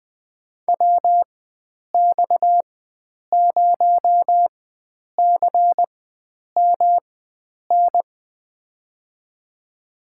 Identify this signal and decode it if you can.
Morse code, heard as a keyed tone, about 20 wpm, spaced as for 9 wpm, 700 Hz. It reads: WX0CMN
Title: USA Call Signs - 4x Character Spacing - 20wpm